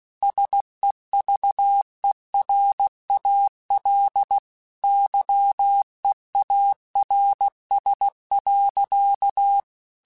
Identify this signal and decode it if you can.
Morse code, heard as a keyed tone, about 16 wpm, 800 Hz. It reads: SEVERAL YEARS.
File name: QSOs / QSOs-mp3s / kc7haa@15